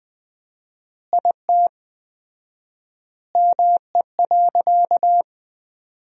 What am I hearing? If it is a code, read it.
Morse code, heard as a keyed tone, about 20 wpm, 700 Hz. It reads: IT ME.
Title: 4x Word-Spacing - Sentences from Top 200 Words - 20wpm